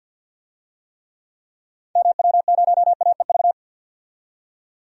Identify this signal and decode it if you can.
Morse code, heard as a keyed tone, about 50 wpm, 700 Hz. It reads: MW0AEV
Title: DXCC-100 Call Signs - 6 Characters - 50wpm